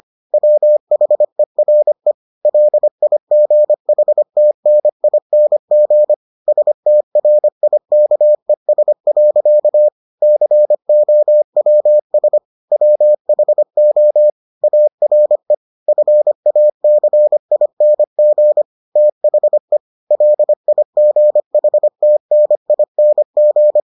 Morse code, heard as a keyed tone, about 25 wpm, 600 Hz. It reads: WHERE LIGHTNING STRIKES. COWS WHO ARE FACING THE LIGHTNING